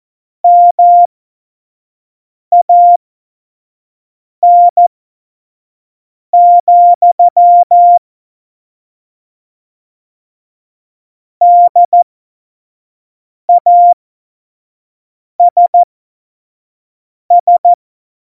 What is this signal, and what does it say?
Morse code, heard as a keyed tone, about 14 wpm, spaced as for 5 wpm, 700 Hz. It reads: MAN, DASS
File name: Morsekurs / Witze / Witz 08. Januar 2019, 25 F70